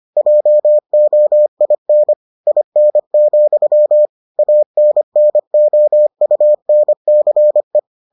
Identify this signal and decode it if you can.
Morse code, heard as a keyed tone, about 25 wpm, 600 Hz. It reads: JOIN IN, ANNOUNCE